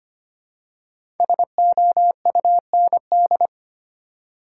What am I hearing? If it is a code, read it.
Morse code, heard as a keyed tone, about 25 wpm, 700 Hz. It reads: SOUND
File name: 5 Letter Words - First 3 Letters Given - 25wpm